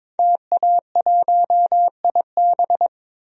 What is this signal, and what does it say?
Morse code, heard as a keyed tone, about 22 wpm, 700 Hz. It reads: TA1IB
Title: DXCC-100 Call Signs - 5 Characters - 22wpm